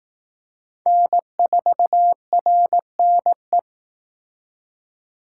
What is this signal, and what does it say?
Morse code, heard as a keyed tone, about 18 wpm, 700 Hz. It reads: N4RNE
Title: Speed Racing - USA 1x3 Call Signs 15wpm